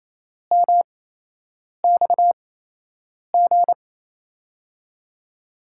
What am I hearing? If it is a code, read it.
Morse code, heard as a keyed tone, about 28 wpm, spaced as for 8 wpm, 700 Hz. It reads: MXG
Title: USA Call Signs - 5 Characters - 8x Character Spacing - 28wpm